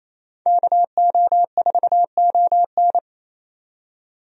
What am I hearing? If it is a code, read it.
Morse code, heard as a keyed tone, about 28 wpm, 700 Hz. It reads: KO4ON
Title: USA Call Signs - 5 Characters - 28wpm